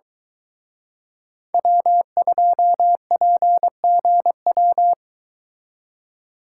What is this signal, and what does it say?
Morse code, heard as a keyed tone, about 22 wpm, 700 Hz. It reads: W2PGW